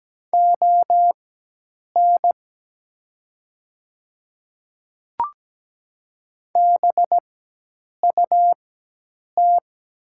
Morse code, heard as a keyed tone, about 17 wpm, spaced as for 8 wpm, 700 Hz. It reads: ON BUT
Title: 4x Character Spacing - Top 100 Words - 17wpm